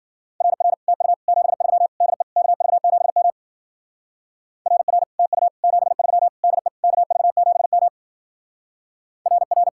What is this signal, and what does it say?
Morse code, heard as a keyed tone, about 48 wpm, 700 Hz. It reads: RR TU 73 DE KV7M RR TU 73 DE KV7M RR
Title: Speed-Racing Common QSO Phrases - 40wpm